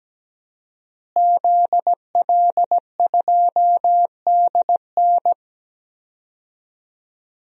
Morse code, heard as a keyed tone, about 17 wpm, 700 Hz. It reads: ZL2DN
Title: DXCC-100 Call Signs - 5 Characters - Repeated 2x - 17wpm